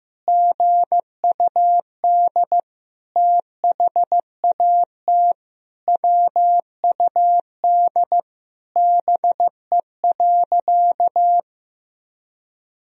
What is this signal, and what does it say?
Morse code, heard as a keyed tone, about 15 wpm, 700 Hz. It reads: GUD THAT WUD BE.
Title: Sentences from Top 200 Words and Abbreviations 15wpm